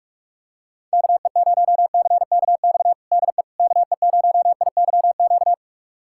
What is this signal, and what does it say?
Morse code, heard as a keyed tone, about 46 wpm, 700 Hz. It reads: KE0CKX DE KE0IYQ
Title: USA Callsign DE Callsign - 6 Characters - 45wpm